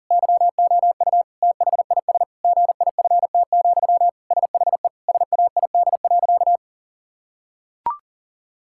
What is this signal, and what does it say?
Morse code, heard as a keyed tone, about 40 wpm, 690 Hz. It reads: YOU THIS GIFT, SHE SAID.
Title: Sentences from Top 1000 Words - Review Part 6 - 40wpm